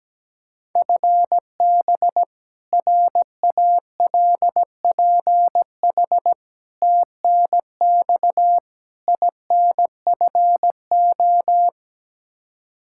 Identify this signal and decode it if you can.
Morse code, heard as a keyed tone, about 17 wpm, 700 Hz. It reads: FB RALPH TNX INFO
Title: Practice QSO - Exchange 3 - More Info - 17wpm